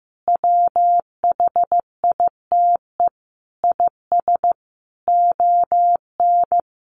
Morse code, heard as a keyed tone, about 15 wpm, 700 Hz. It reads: WHITE IS ON
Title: Sentences from Top 1500 Words 15wpm